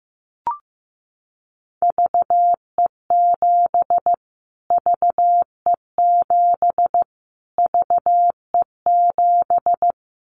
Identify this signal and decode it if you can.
Morse code, heard as a keyed tone, about 15 wpm, 700 Hz. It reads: VE7 VE7 VE7
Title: Callsign Prefixes for Top 25 DXCC - Repeated 3x - 15wpm